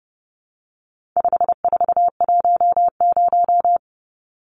Morse code, heard as a keyed tone, about 30 wpm, 700 Hz. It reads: 5410
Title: Numbers - 4-Digit 30wpm